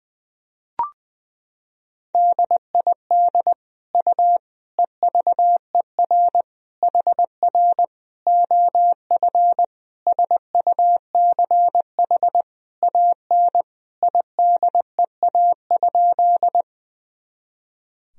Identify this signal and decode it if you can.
Morse code, heard as a keyed tone, about 21 wpm, 700 Hz. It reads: DID U EVER HR OF SUCH AN IDEA?
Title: Sentences from Top 400 Words and Abbreviations 20wpm